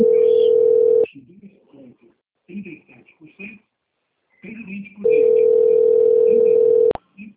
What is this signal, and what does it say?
Ringback tone (North America), 440 + 480 Hz, with speech.
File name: out- - - - - .7456